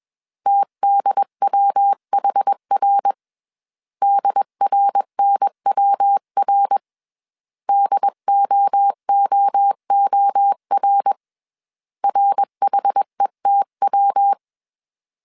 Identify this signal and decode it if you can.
Morse code, heard as a keyed tone, about 21 wpm, 800 Hz. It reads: TDWHR DRNWR DOOOR RHETW